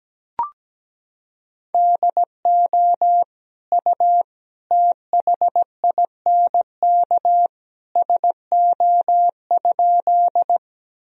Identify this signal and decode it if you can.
Morse code, heard as a keyed tone, about 17 wpm, 700 Hz. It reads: DO U THINK SO?
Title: Sentences from Top 200 Words and Abbreviations 17wpm